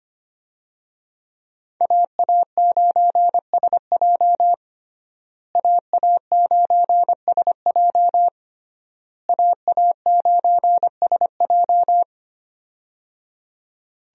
Morse code, heard as a keyed tone, about 25 wpm, 700 Hz. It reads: AA9SJ AA9SJ AA9SJ
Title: USA Call Signs - 5 Characters - Repeated 3x - 25wpm